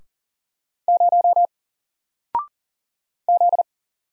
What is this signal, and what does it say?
Morse code, heard as a keyed tone, about 40 wpm, 700 Hz. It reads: 0 Z
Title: Single Letter-Number 40wpm